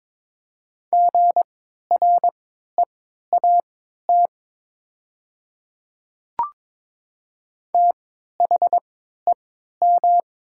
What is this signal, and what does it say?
Morse code, heard as a keyed tone, about 23 wpm, spaced as for 13 wpm, 700 Hz. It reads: GREAT THEM
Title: 3x Character Spacing - Top 200 Words - 22wpm